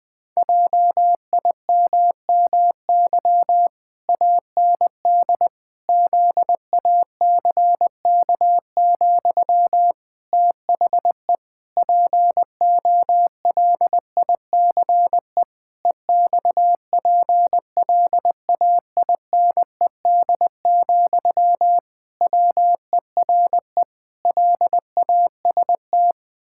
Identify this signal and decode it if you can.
Morse code, heard as a keyed tone, about 20 wpm, 700 Hz. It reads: JIMMY AND ZACK, THE POLICE EXPLAINED, WERE LAST
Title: Pangram Challenge v2 - 20wpm